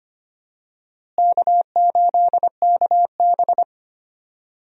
Morse code, heard as a keyed tone, about 25 wpm, 700 Hz. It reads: K8KB
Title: USA Call Signs - 4 Characters - Repeated 2x - 25wpm